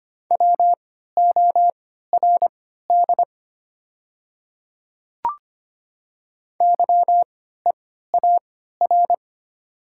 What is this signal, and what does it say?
Morse code, heard as a keyed tone, about 25 wpm, spaced as for 14 wpm, 700 Hz. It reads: WORD YEAR
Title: 3x Character Spacing - Top 200 Words - 25wpm